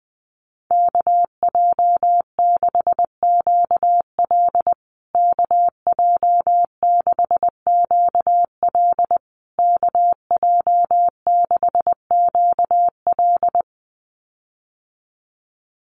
Morse code, heard as a keyed tone, about 20 wpm, 700 Hz. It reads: KJ6QL KJ6QL KJ6QL
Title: USA Call Signs Repeated 3x 20wpm